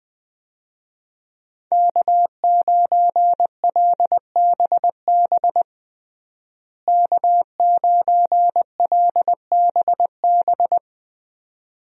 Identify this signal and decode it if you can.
Morse code, heard as a keyed tone, about 21 wpm, 700 Hz. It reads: K9LBB K9LBB
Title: USA Call Signs - 5 Characters - Repeated 2x - 20wpm